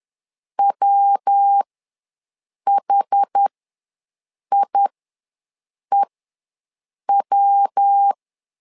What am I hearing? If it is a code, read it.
Morse code, heard as a keyed tone, about 10 wpm, spaced as for 6 wpm, 800 Hz. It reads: WHIEW
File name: LEZIONI / 3 MWRO / 3 MW ws 10wpm 2213